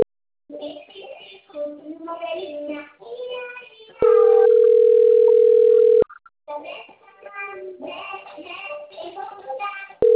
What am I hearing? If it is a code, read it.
Ringback tone (North America), 440 + 480 Hz, with speech.
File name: rg-1600-unknown- - - .20283